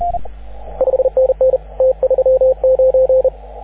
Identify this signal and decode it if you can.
Morse code, heard as a keyed tone, about 37 wpm, 530 Hz, with radio noise. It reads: 5NN T39
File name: AUDIO2019 / HG6N CW SSB MIX